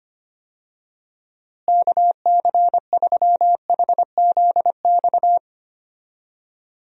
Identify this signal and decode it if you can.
Morse code, heard as a keyed tone, about 25 wpm, 700 Hz. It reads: KC3HZX
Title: USA Call Signs - 6 Characters - Repeated 3x - 25wpm